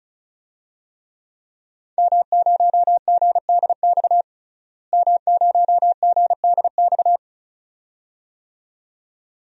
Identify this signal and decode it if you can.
Morse code, heard as a keyed tone, about 34 wpm, 700 Hz. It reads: M0GDX M0GDX
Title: DXCC-100 Call Signs - 5 Characters - Repeated 2x - 35wpm